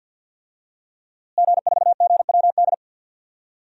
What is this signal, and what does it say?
Morse code, heard as a keyed tone, about 50 wpm, 700 Hz. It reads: G4GWD
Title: DXCC-100 Call Signs - 5 Characters - Repeated 2x - 50wpm